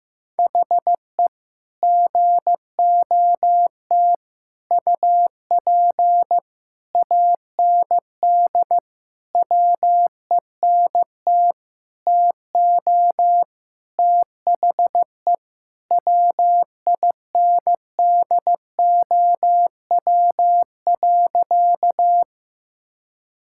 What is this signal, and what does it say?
Morse code, heard as a keyed tone, about 15 wpm, 700 Hz. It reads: HE GOT UP AND WENT TO THE WINDOW.
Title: Sentences from Top 1600 Words 15wpm